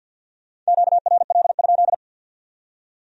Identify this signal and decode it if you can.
Morse code, heard as a keyed tone, about 64 wpm, 700 Hz. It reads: QRL?